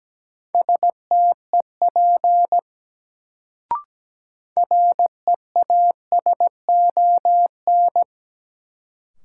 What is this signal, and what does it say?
Morse code, heard as a keyed tone, about 17 wpm, 700 Hz. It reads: STEP REASON
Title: Sets of 5 Words Encore - New Word Drill - 17wpm